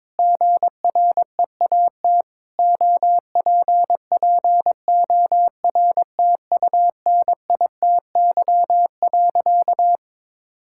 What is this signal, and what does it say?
Morse code, heard as a keyed tone, about 23 wpm, 700 Hz. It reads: GREAT OPPORTUNITY.
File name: practice / 22wpm / Sentences with 4-Word Phrases - 22wpm